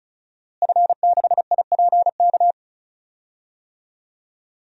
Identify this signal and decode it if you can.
Morse code, heard as a keyed tone, about 34 wpm, 700 Hz. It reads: F6IPK
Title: DXCC-100 Call Signs - 5 Characters - Repeated 2x - 35wpm